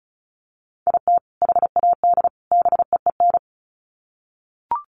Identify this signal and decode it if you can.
Morse code, heard as a keyed tone, about 34 wpm, 700 Hz. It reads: IT HAD BEEN
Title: Sets of 3 Words 35wpm